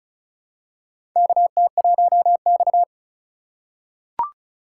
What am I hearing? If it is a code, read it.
Morse code, heard as a keyed tone, about 36 wpm, 700 Hz. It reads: KT1X
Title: USA Call Signs - 2x Character Spacing - 35wpm